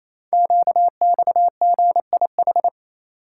Morse code, heard as a keyed tone, about 28 wpm, 700 Hz. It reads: QXGIH
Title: Code Groups - 5 Letters - 28wpm